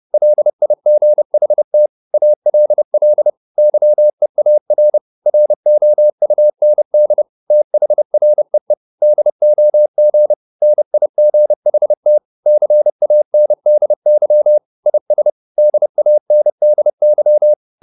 Morse code, heard as a keyed tone, about 30 wpm, 600 Hz. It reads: LIGHT ALL YEAR ROUND THREE DOG NIGHT CANDY IS DANDY